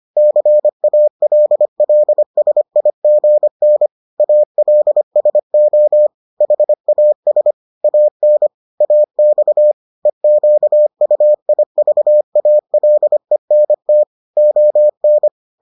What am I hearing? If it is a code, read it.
Morse code, heard as a keyed tone, about 25 wpm, 600 Hz. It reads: CALLSIGN ALSO HAS AN AX EQUIVALENT ON